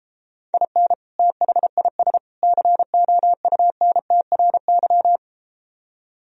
Morse code, heard as a keyed tone, about 33 wpm, 700 Hz. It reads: IN THIS COUNTRY